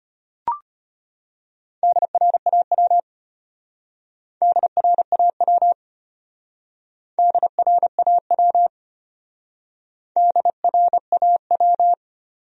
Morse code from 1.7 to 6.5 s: DRAW DRAW

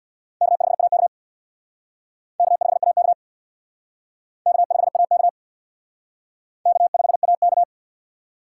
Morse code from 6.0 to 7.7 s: K5AK

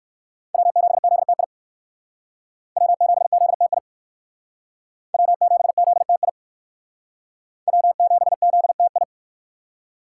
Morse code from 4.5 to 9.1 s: W7ZTI W7ZTI

